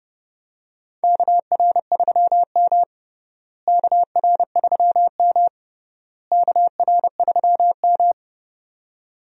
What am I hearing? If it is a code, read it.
Morse code, heard as a keyed tone, about 30 wpm, 700 Hz. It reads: KR3M KR3M KR3M
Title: USA Call Signs - 4 Characters - Repeated 3x - 30wpm